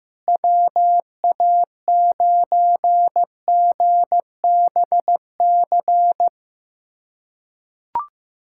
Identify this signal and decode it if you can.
Morse code, heard as a keyed tone, about 15 wpm, 700 Hz. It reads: WA9GBC
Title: USA Callsign DE Callsign - 6 Characters - 15wpm